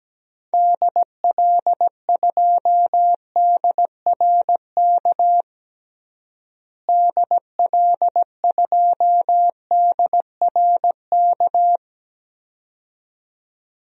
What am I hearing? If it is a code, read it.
Morse code, heard as a keyed tone, about 17 wpm, 700 Hz. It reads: DL2DRK DL2DRK
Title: DXCC-100 Call Signs - 6 Characters - Repeated 2x - 17wpm